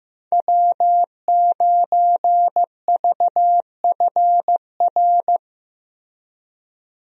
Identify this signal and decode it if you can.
Morse code, heard as a keyed tone, about 15 wpm, 700 Hz. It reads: W9VFR
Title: USA Call Signs - 5 Characters - Repeated 3x - 15wpm